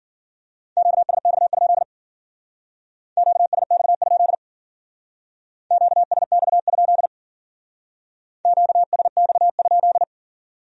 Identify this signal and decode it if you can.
Morse code, heard as a keyed tone, about 49 wpm, 700 Hz. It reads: QSX? QSX? QSX? QSX?
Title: Speed-Racing Q-Codes - 40wpm